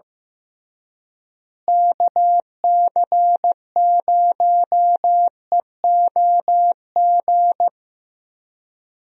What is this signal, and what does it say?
Morse code, heard as a keyed tone, about 15 wpm, 700 Hz. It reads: KC0EOG
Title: USA Call Signs - 6 Characters - Repeated 3x - 15wpm